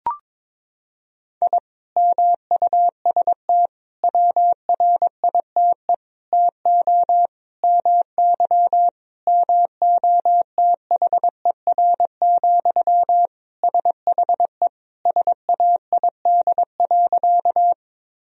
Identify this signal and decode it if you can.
Morse code, heard as a keyed tone, about 22 wpm, 700 Hz. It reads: I MUST WRITE TO MY MOTHER, SHE SAID.
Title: Sentences from Top 1000 Words - Review Part 1 - 22wpm